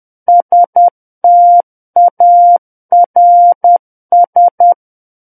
Morse code, heard as a keyed tone, about 10 wpm, 700 Hz. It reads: STARS
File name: Book One / 01-Chapter 1 - 10 WPM